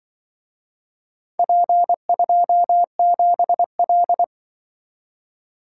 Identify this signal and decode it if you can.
Morse code, heard as a keyed tone, about 24 wpm, 700 Hz. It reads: P27L
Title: Speed-Racing - Four Letter-Number Combinations - 20wpm